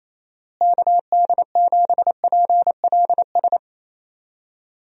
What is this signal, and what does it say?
Morse code, heard as a keyed tone, about 28 wpm, 700 Hz. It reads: KD7PLS
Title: USA Call Signs - 6 Characters - Repeated 2x - 28wpm